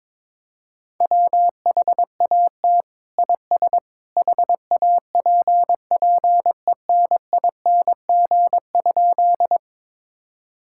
Morse code, heard as a keyed tone, about 22 wpm, 700 Hz. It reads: WHAT IS HAPPENING?
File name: practice / 22wpm / Sentences with 2-Word Phrases - 22wpm